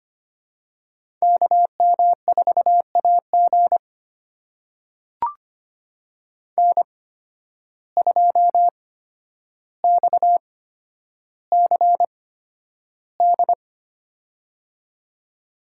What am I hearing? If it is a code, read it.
Morse code, heard as a keyed tone, about 25 wpm, spaced as for 11 wpm, 700 Hz. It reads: KM4AG N2XCD
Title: USA Call Signs - 5 Characters - 8x Character Spacing - 25wpm